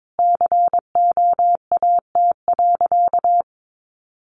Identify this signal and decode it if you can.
Morse code, heard as a keyed tone, about 22 wpm, 700 Hz. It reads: COAT.
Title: Additional Easy Phrases 22wpm